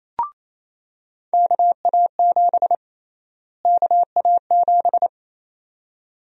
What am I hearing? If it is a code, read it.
Morse code, heard as a keyed tone, about 28 wpm, 700 Hz. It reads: KA7 KA7
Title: USA Call Sign Prefixes - Repeated 2x - 28wpm